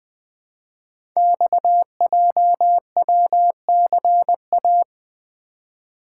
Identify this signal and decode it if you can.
Morse code, heard as a keyed tone, about 20 wpm, 700 Hz. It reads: XJWCA
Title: Code Groups - 5 Letter:Numbers - 20wpm